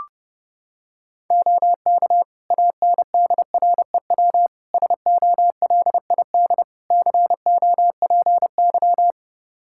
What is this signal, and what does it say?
Morse code, heard as a keyed tone, about 30 wpm, 700 Hz. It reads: OK ANDREW SOLID COPY
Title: Practice QSO - Exchange 3 - More Info - 30wpm